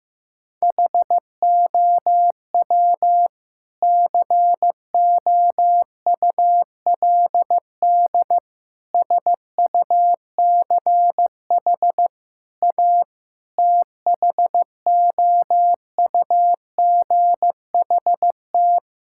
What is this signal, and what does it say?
Morse code, heard as a keyed tone, about 15 wpm, 700 Hz. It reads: HOW COULD SUCH A THOUGHT
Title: Sentences from Top 1000 Words - Review Part 6 - 15wpm